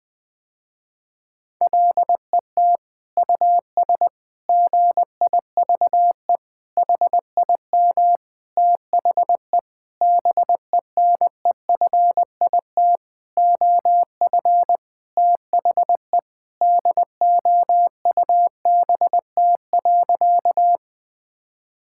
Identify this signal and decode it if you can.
Morse code, heard as a keyed tone, about 20 wpm, 700 Hz. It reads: LET US GIVE HIM THE BENEFIT OF THE DOUBT.
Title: Sentences with 7-Word Phrases - Spoken Beginning - 20wpm